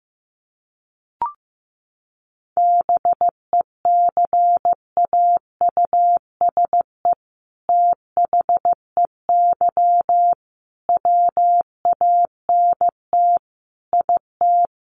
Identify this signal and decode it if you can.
Morse code, heard as a keyed tone, about 15 wpm, 700 Hz. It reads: BECAUSE THEY WANT IT
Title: Sentences from Top 800 Words 15wpm